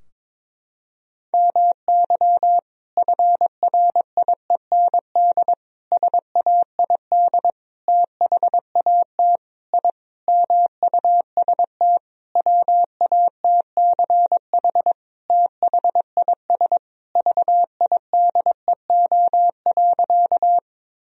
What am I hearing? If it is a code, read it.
Morse code, heard as a keyed tone, about 22 wpm, 700 Hz. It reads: MY FRIEND SAID THAT I MUST WATCH THIS VIDEO.
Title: Sentences from Top 1000 Words - Review Part 4 - 22wpm